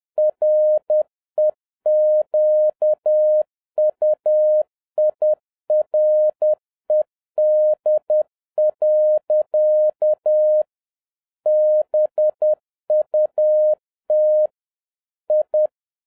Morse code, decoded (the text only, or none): REQUIRED. BUT I